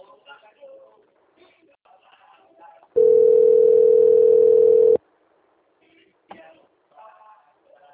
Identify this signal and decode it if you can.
Ringback tone (North America), 440 + 480 Hz, with speech.
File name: rg-1600-unknown- - - .21352